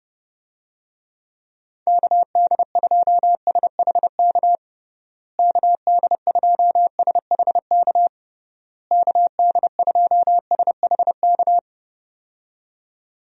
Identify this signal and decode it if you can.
Morse code, heard as a keyed tone, about 31 wpm, 700 Hz. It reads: KD2SHK KD2SHK KD2SHK
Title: USA Call Signs - 6 Characters - Repeated 3x - 30wpm